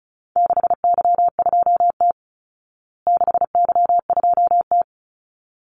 Morse code, heard as a keyed tone, about 34 wpm, 700 Hz. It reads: 6Y2T 6Y2T
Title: Call Signs Repeated 3x - 35wpm